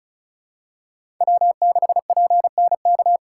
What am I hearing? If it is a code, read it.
Morse code, heard as a keyed tone, about 36 wpm, 700 Hz. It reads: W6PNK
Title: USA Call Signs - 3x Character Spacing - 35wpm